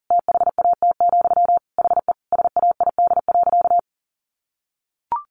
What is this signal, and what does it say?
Morse code, heard as a keyed tone, about 40 wpm, 700 Hz. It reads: THAT, HE SAID.
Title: Sentences from Top 1000 Words 40wpm